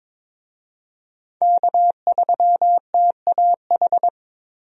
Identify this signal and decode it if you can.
Morse code, heard as a keyed tone, about 22 wpm, 700 Hz. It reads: K3TAH
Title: USA Call Signs - 5 Characters - 22wpm